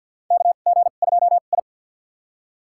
Morse code, heard as a keyed tone, about 53 wpm, spaced as for 37 wpm, 700 Hz. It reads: KG2I
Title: USA Call Signs - 4 Characters - 2x Character Spacing - 50wpm